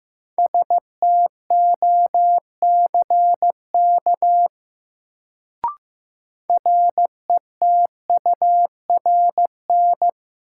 Morse code, heard as a keyed tone, about 15 wpm, 700 Hz. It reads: STOCK RETURN
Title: Sentences from Top 800 Words - New Word Drill - 15wpm